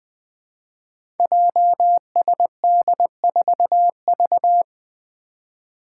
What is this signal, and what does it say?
Morse code, heard as a keyed tone, about 20 wpm, 700 Hz. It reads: JSD4V
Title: Five Challenging Letter-Number Combinations 20wpm